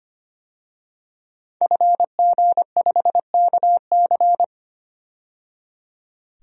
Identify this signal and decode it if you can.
Morse code, heard as a keyed tone, about 25 wpm, 700 Hz. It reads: FG5KC
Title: DXCC-100 Call Signs - 5 Characters - 25wpm